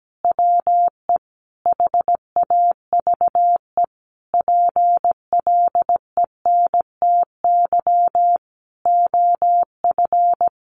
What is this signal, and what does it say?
Morse code, heard as a keyed tone, about 17 wpm, 700 Hz. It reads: WE HAVE PLENTY OF
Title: Sentences from Top 1300 Words 17wpm